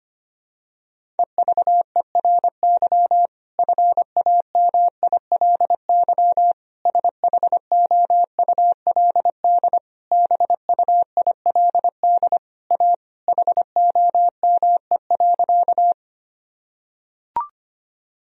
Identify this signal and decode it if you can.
Morse code, heard as a keyed tone, about 25 wpm, 700 Hz. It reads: EVERY FAMILY SHOULD BUILD A HOME.
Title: Sentences from Top 1000 Words - Review Part 1 - 25wpm